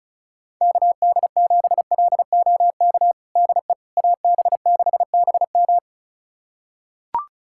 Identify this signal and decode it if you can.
Morse code, heard as a keyed tone, about 34 wpm, 690 Hz. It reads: KD7LOK DE AB6BM
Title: USA Callsign DE Callsign 35wpm